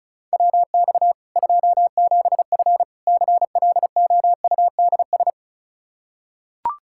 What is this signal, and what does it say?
Morse code, heard as a keyed tone, about 34 wpm, 700 Hz. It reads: WX 27F CLOUDS
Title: K1USN SST QSO Practice 35wpm